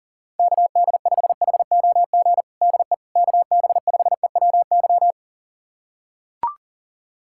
Morse code, heard as a keyed tone, about 40 wpm, 700 Hz. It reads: KD5HOG DE KB5EWY
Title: USA Callsign DE Callsign - 6 Characters - 40wpm